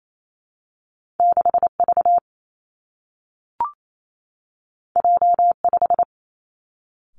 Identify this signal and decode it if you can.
Morse code, heard as a keyed tone, about 28 wpm, 700 Hz. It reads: 6 V J 5
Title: Two Challenging Letter-Number Combinations 28wpm